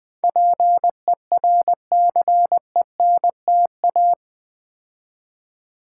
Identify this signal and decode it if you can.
Morse code, heard as a keyed tone, about 20 wpm, 700 Hz. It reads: PERCENTA
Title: Build A Word - 4 Letters At A Time - 10-Letter Words - 20wpm